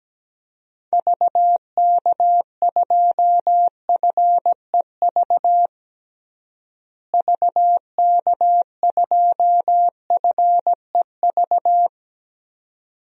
Morse code, heard as a keyed tone, about 17 wpm, 700 Hz. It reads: VK2FEV VK2FEV
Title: DXCC-100 Call Signs - 6 Characters - Repeated 2x - 17wpm